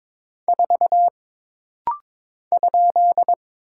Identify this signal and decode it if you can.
Morse code, heard as a keyed tone, about 22 wpm, 700 Hz. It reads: 4 ?
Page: This rapid-fire timing is geared to push your ICR (Instant Character Recognition) skills.